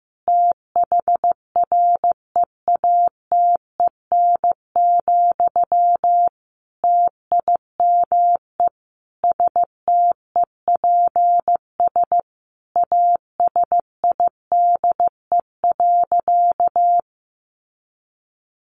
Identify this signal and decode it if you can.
Morse code, heard as a keyed tone, about 15 wpm, 700 Hz. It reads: THREATEN, TIME STEPS ASIDE.
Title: A Sound Of Thunder 15wpm